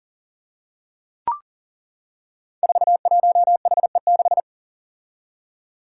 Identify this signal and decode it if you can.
Morse code, heard as a keyed tone, about 39 wpm, 700 Hz. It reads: 41HE6
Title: Five Challenging Letter-Number Combinations 40wpm